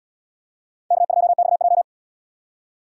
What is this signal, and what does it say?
Morse code, heard as a keyed tone, about 77 wpm, 700 Hz. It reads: K2XJ